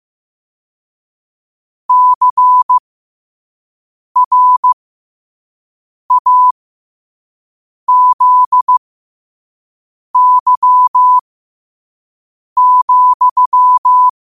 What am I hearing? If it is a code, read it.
Morse code, heard as a keyed tone, about 15 wpm, spaced as for 5 wpm, 1000 Hz. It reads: CRAZY,
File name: qotdfiles / 5wpm